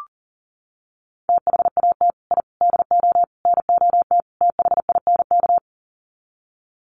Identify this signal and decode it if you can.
Morse code, heard as a keyed tone, about 40 wpm, 700 Hz. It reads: THAT I DO NOT THINK
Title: Sets of 5 Words Encore 40wpm